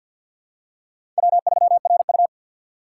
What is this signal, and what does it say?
Morse code, heard as a keyed tone, about 47 wpm, 690 Hz. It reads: W3RU
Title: USA Call Signs - 4 Characters - 6x Character Spacing - 50wpm